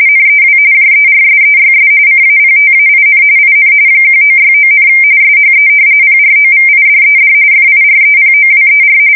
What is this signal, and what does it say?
RTTY, about 45.45 baud. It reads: NEW JERSEYS CAPTAIN IN 1969, SHE PROVIDED FIREPOWER
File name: fsk170 / battleshipnj